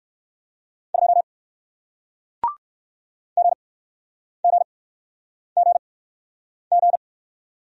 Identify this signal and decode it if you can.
Morse code, heard as a keyed tone, about 59 wpm, spaced as for 9 wpm, 700 Hz. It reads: 2 GGGG